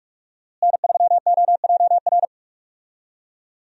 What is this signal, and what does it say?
Morse code, heard as a keyed tone, about 41 wpm, 700 Hz. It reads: N3OJR
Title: USA Call Signs - 5 Characters - Repeated 2x - 45wpm